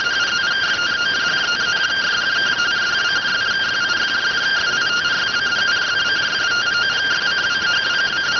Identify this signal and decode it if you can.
Two-tone teleprinter signal, about 170 Hz shift.